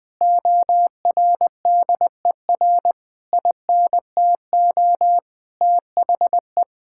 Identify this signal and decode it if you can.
Morse code, heard as a keyed tone, about 20 wpm, 700 Hz. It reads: ORDER INTO THE
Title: Sentences from Top 1600 Words 20wpm